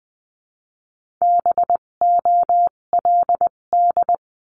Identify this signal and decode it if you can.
Morse code, heard as a keyed tone, about 20 wpm, spaced as for 17 wpm, 700 Hz. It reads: BOLD
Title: 257 Lesson 42b - Words 14-20wpm